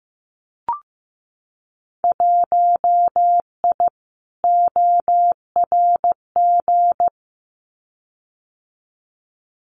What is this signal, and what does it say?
Morse code, heard as a keyed tone, about 15 wpm, 700 Hz. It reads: 1I ORG
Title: Winter Field Day Exchanges 15wpm